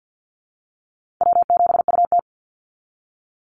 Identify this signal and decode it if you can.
Morse code, heard as a keyed tone, about 50 wpm, 700 Hz. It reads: W7UT
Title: Easy Call Signs Repeated 2x - 50wpm